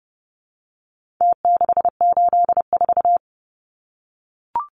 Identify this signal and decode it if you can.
Morse code, heard as a keyed tone, about 30 wpm, 700 Hz. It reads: T684
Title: Numbers - Contest Style - 4-Digit 30wpm